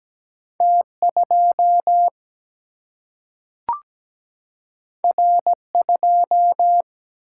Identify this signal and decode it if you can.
Morse code, heard as a keyed tone, about 17 wpm, 700 Hz. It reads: T 2 R 2
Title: Single-Letter Single-Number 17wpm